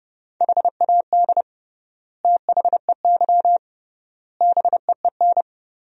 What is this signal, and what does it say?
Morse code, heard as a keyed tone, about 30 wpm, 700 Hz. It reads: HAD THEY BEEN
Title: 3x Word-Spacing - Sentences from Top 400 Words - 30wpm